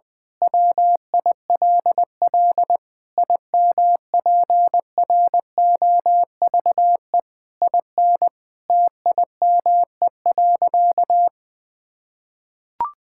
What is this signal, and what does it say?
Morse code, heard as a keyed tone, about 20 wpm, 700 Hz. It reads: WILL IMPROVE IN TIME.
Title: Sentences from Top 1000 Words - Review Part 5 - 20wpm